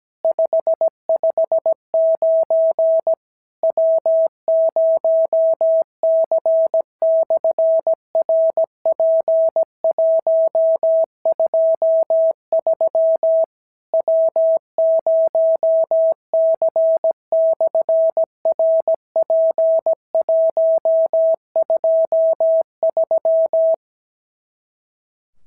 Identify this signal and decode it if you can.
Morse code, heard as a keyed tone, about 17 wpm, 650 Hz. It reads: 559 W0C/RP123 W0C/RP123
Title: SOTA QSO Practice 17wpm